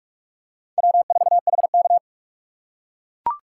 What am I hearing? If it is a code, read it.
Morse code, heard as a keyed tone, about 41 wpm, 700 Hz. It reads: W4HK
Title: USA Call Signs - 4 Characters - 6x Character Spacing - 45wpm